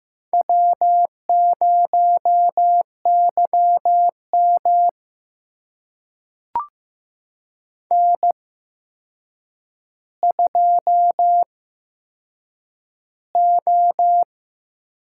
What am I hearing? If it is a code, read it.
Morse code, heard as a keyed tone, about 15 wpm, spaced as for 7 wpm, 700 Hz. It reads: W0YM N2O
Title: USA Call Signs - 8x Character Spacing - 15wpm